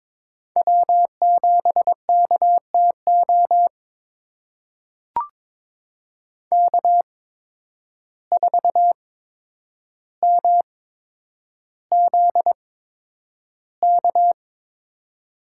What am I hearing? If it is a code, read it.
Morse code, heard as a keyed tone, about 22 wpm, spaced as for 9 wpm, 700 Hz. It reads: W7KTO K4MZK
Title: USA Call Signs - 5 Characters - 8x Character Spacing - 22wpm